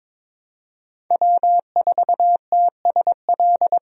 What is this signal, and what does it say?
Morse code, heard as a keyed tone, about 22 wpm, 700 Hz. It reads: W4TSL
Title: USA Call Signs - 5 Characters - Repeated 2x - 22wpm